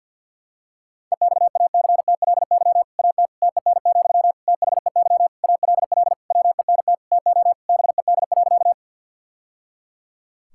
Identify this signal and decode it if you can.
Morse code, heard as a keyed tone, about 47 wpm, 700 Hz. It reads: EXACTLY AT TEN, THEY ALL WENT TO BED.
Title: Sentences from Top 1000 Words - Review Part 6 - 50wpm